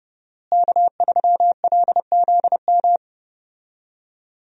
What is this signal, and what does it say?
Morse code, heard as a keyed tone, about 29 wpm, 700 Hz. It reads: K3LZM
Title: USA Call Signs - 5 Characters - Repeated 3x - 30wpm